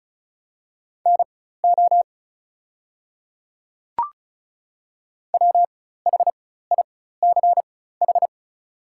Morse code, heard as a keyed tone, about 36 wpm, spaced as for 16 wpm, 700 Hz. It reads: NO WHICH
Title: 4x Character Spacing - Top 100 Words - 35wpm